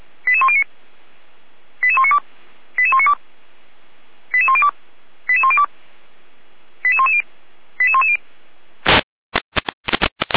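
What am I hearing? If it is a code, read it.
Five-tone selective call (ZVEI), heard as quick runs of five tones, tones of about 70 ms. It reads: 88109 88192 88192 88102 88102 88110 88110
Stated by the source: Five-tone ZVEI selective call received by SDR, from the audio server of Feuerwehr Orlamünde (fire brigade).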